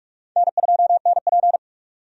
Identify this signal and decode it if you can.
Morse code, heard as a keyed tone, about 47 wpm, 700 Hz. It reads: N2NP
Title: USA Call Signs - 4 Characters - 45wpm